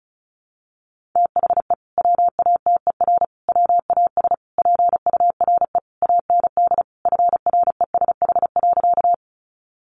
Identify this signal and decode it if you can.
Morse code, heard as a keyed tone, about 34 wpm, 700 Hz. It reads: THE WATER WAS PURE AND FRESH.
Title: Sentences from Top 1500 Words 35wpm